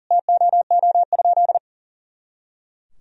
Morse code, heard as a keyed tone, about 40 wpm, 700 Hz. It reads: TOO?
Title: Sentences from Top 500 Words and Abbreviations 40wpm